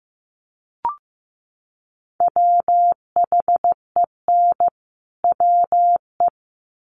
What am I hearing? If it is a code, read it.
Morse code, heard as a keyed tone, about 15 wpm, 700 Hz. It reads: WHEN WE
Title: Sets of 2 Words 15wpm